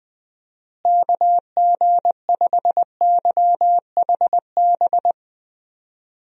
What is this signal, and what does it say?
Morse code, heard as a keyed tone, about 20 wpm, 700 Hz. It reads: KG5YHB
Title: USA Call Signs - 6 Characters - Repeated 2x - 20wpm